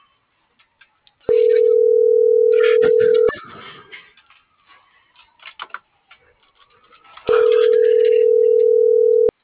Ringback tone (North America), 440 + 480 Hz.